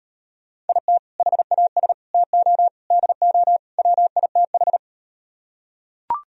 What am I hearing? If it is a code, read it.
Morse code, heard as a keyed tone, about 38 wpm, 700 Hz. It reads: IT HAS TO DO WITH